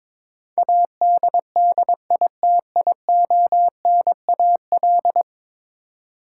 Morse code, heard as a keyed tone, about 22 wpm, 700 Hz. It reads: ADDITIONAL
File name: practice / 22wpm / 10 Letter Words - First 3 Letters Given - 22wpm